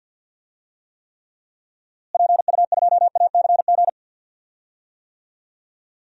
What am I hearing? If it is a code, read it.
Morse code, heard as a keyed tone, about 52 wpm, 700 Hz. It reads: PU2ACG
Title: DXCC-100 Call Signs - 6 Characters - Repeated 2x - 50wpm